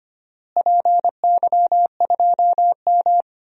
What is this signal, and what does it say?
Morse code, heard as a keyed tone, about 25 wpm, 700 Hz. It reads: PY2M
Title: DXCC-100 Call Signs - 4 Characters - Repeated 3x - 25wpm